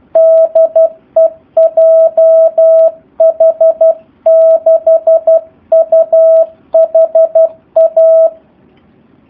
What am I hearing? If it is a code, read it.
Morse code, heard as a keyed tone, about 12 wpm, 650 Hz. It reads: DE JH6UHA
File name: jh8cbh / 2019cw / SP04- w